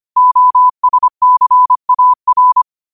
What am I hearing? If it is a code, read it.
Morse code, heard as a keyed tone, about 25 wpm, 1000 Hz. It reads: OSCAR
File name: qotdfiles / 25wpm